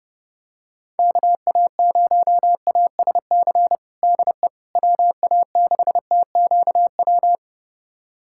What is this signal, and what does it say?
Morse code, heard as a keyed tone, about 32 wpm, 700 Hz. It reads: KA0ASC DE WA6TQW
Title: USA Callsign DE Callsign - 6 Characters - 30wpm